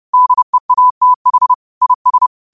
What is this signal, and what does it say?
Morse code, heard as a keyed tone, about 30 wpm, 1000 Hz. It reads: DEATH IS